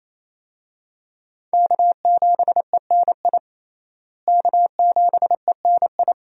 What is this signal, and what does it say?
Morse code, heard as a keyed tone, about 28 wpm, 700 Hz. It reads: K7ENI K7ENI
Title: USA Call Signs - 5 Characters - Repeated 2x - 28wpm